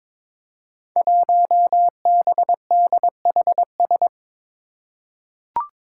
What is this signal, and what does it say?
Morse code, heard as a keyed tone, about 22 wpm, 700 Hz. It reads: 1BDHS
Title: Five Challenging Letter-Number Combinations 22wpm